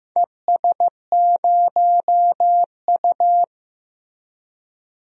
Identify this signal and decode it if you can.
Morse code, heard as a keyed tone, about 15 wpm, 700 Hz. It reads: ES0U
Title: DXCC-100 Call Signs - 4 Characters - Repeated 3x - 15wpm